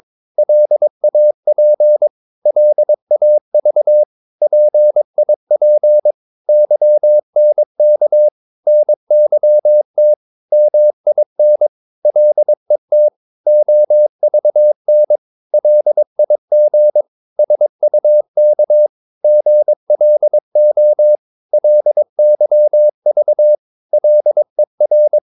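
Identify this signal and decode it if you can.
Morse code, heard as a keyed tone, about 22 wpm, 600 Hz. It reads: LAP LAV PIP YNK NYT MIN LET OVN LIG SUK GLO LYV LER
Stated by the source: Morse practice of short Danish words at 22 wpm, from CW med Gnister.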